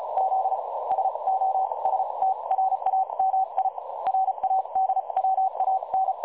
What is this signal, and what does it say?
Morse code, heard as a keyed tone, about 36 wpm, spaced as for 30 wpm, 760 Hz. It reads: Z SOSTRAMI RADWAN
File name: dlazaawansowanych / nagraniazpasma / websdr recording 3565.0kHz